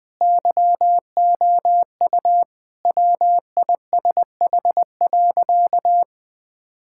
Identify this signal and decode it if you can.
Morse code, heard as a keyed tone, about 21 wpm, 700 Hz. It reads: YOU WISH.
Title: Sentences from Top 1600 Words 20wpm